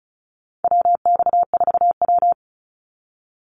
Morse code, heard as a keyed tone, about 34 wpm, 700 Hz. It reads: WX4W
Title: Call Signs Repeated 2x - 35wpm